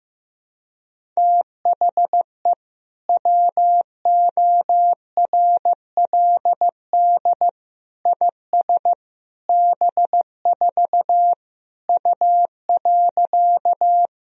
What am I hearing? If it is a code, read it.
Morse code, heard as a keyed tone, about 16 wpm, 700 Hz. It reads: THE WORLD IS B4 U.
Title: Sentences from Top 200 Words and Abbreviations 15wpm